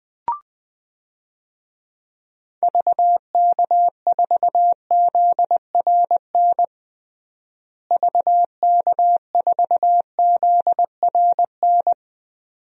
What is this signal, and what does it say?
Morse code, heard as a keyed tone, about 20 wpm, 700 Hz. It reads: VK4ZRN VK4ZRN
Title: DXCC-100 Call Signs - 6 Characters - Repeated 2x - 20wpm